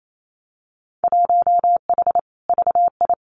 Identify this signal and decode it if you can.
Morse code, heard as a keyed tone, about 28 wpm, 700 Hz. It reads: 1H VI
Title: Winter Field Day Exchanges Encore 28wpm